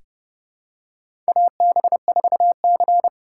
Morse code, heard as a keyed tone, about 30 wpm, 700 Hz. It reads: AB4C
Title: USA Call Signs - 4 Characters - 3x Character Spacing - 30wpm